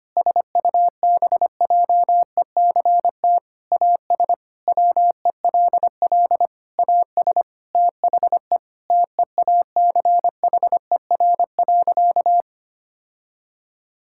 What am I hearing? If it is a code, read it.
Morse code, heard as a keyed tone, about 25 wpm, 700 Hz. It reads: SUBJECT AS WELL AS THE TEACHER.
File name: practice / 25wpm / Sentences with 4-Word Phrases - 25wpm